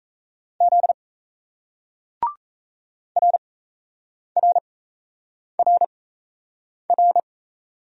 Morse code, heard as a keyed tone, about 34 wpm, spaced as for 8 wpm, 700 Hz. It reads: Z RRRR
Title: Speed-Racing Single Letters 28wpm